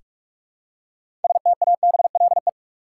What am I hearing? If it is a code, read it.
Morse code, heard as a keyed tone, about 46 wpm, 700 Hz. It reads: STABLE
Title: Six Letter Words - 45wpm